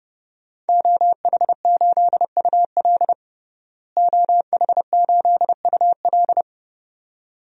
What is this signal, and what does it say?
Morse code, heard as a keyed tone, about 30 wpm, 700 Hz. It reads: OH8UL OH8UL
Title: DXCC-100 Call Signs - 5 Characters - Repeated 2x - 30wpm